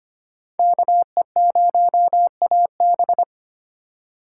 Morse code, heard as a keyed tone, about 31 wpm, 700 Hz. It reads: KE0AB
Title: Speed Racing - USA 2x2 Call Signs 25wpm